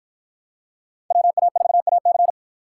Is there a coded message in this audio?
Yes: Morse code from 1.1 to 2.3 s, heard as a keyed tone, about 50 wpm, 700 Hz. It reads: WA4AC